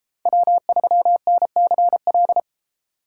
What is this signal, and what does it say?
Morse code, heard as a keyed tone, about 32 wpm, 700 Hz. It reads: W3NCL